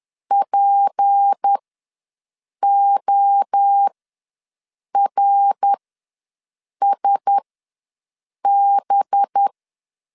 Morse code, heard as a keyed tone, about 10 wpm, spaced as for 6 wpm, 800 Hz. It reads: PORSB
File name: LEZIONI / 7 GF9Z / 7 GF ws 10wpm 3918